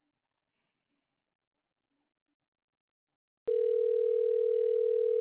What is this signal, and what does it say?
Ringback tone (North America), 440 + 480 Hz.